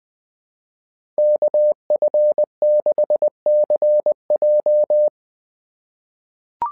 Morse code, heard as a keyed tone, about 20 wpm, 600 Hz. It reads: KF6CJ
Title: SOTA QSO Practice 20wpm